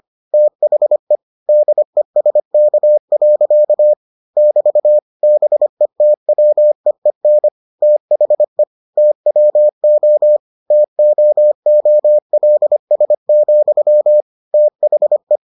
Morse code, heard as a keyed tone, about 25 wpm, 600 Hz. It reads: THE DESK. <BT> BETWEEN THE TWO TOOLS, THE